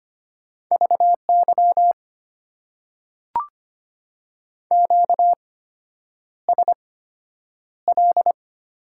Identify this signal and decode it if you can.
Morse code, heard as a keyed tone, about 25 wpm, spaced as for 9 wpm, 700 Hz. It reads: VY QSL
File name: Top 90 QSO Elements - 8x Character Spacing - 25wpm